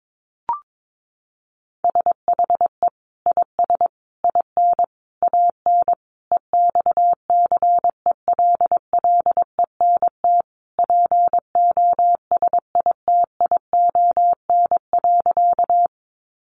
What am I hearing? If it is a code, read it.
Morse code, heard as a keyed tone, about 22 wpm, 700 Hz. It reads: SHE IS IN AN EXCELLENT POSITION.
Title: Sentences from Top 1000 Words 22wpm